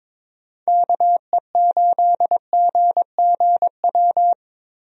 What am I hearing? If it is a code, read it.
Morse code, heard as a keyed tone, about 22 wpm, 700 Hz. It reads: KE8GGW
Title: USA Call Signs - 6 Characters - Repeated 3x - 22wpm